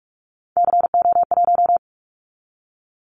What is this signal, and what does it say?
Morse code, heard as a keyed tone, about 46 wpm, 700 Hz. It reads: CO1
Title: Callsign Prefixes for Top 100 DXCC - 45wpm